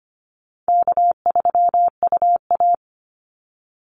Morse code, heard as a keyed tone, about 25 wpm, 700 Hz. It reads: K3UA
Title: Call Signs Repeated 2x - 25wpm